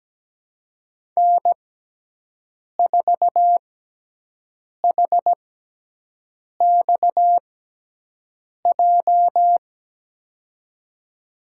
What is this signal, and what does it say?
Morse code, heard as a keyed tone, about 17 wpm, spaced as for 6 wpm, 700 Hz. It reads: N4HXJ
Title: USA Call Signs - 5 Characters - 6x Character Spacing - 17wpm